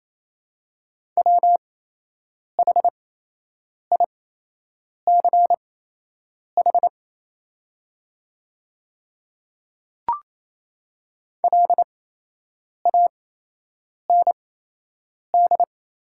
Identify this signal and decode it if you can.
Morse code, heard as a keyed tone, about 28 wpm, spaced as for 8 wpm, 700 Hz. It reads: WHICH LAND
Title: 8x Character Spacing - Top 300 Words - 28wpm